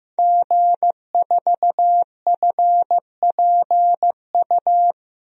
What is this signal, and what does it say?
Morse code, heard as a keyed tone, about 15 wpm, 700 Hz. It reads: G4FPU
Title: DXCC-100 Call Signs - 5 Characters - Repeated 2x - 15wpm